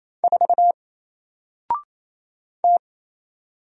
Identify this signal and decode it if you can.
Morse code, heard as a keyed tone, about 29 wpm, 700 Hz. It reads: 4 T
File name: Single-Letter-Number-28wpm